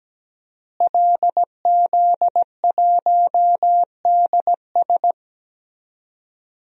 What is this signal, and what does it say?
Morse code, heard as a keyed tone, about 17 wpm, 700 Hz. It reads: LZ1DS
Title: DXCC-100 Call Signs - 5 Characters - Repeated 2x - 17wpm